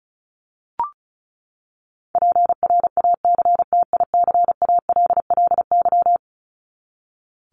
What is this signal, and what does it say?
Morse code, heard as a keyed tone, about 34 wpm, 700 Hz. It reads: PRACTICALLY
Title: LY Suffix - Top 100 Words with Base Words 35wpm